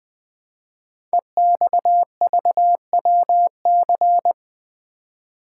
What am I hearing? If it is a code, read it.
Morse code, heard as a keyed tone, about 20 wpm, 700 Hz. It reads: EXVWC
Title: Code Groups - 5 Letters - 20wpm